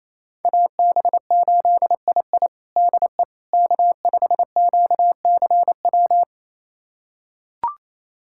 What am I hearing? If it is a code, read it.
Morse code, heard as a keyed tone, about 28 wpm, 700 Hz. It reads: AB8II DE K5QCW
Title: USA Callsign DE Callsign - 5 Characters - 28wpm